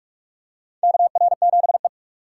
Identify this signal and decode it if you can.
Morse code, heard as a keyed tone, about 44 wpm, 700 Hz. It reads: KR7E
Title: USA Call Signs - 4 Characters - 6x Character Spacing - 45wpm